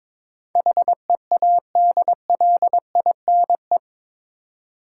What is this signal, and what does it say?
Morse code, heard as a keyed tone, about 22 wpm, 700 Hz. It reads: HEADLINE
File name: practice / 22wpm / Eight Letter Words - 22wpm